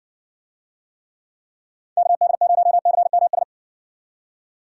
Morse code, heard as a keyed tone, about 60 wpm, 700 Hz. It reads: KD0CMS